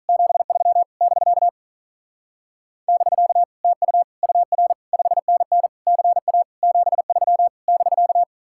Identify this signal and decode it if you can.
Morse code, heard as a keyed tone, about 44 wpm, 700 Hz. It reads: 73 <BK> <BK> TU UR 5NN CA 73 <BK>
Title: Speed-Racing Common QSO Phrases - 35wpm